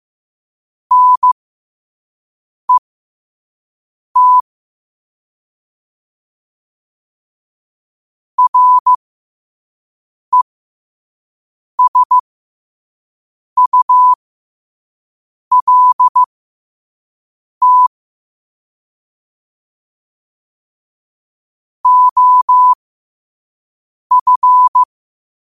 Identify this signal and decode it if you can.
Morse code, heard as a keyed tone, about 15 wpm, spaced as for 5 wpm, 1000 Hz. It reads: NET RESULT OF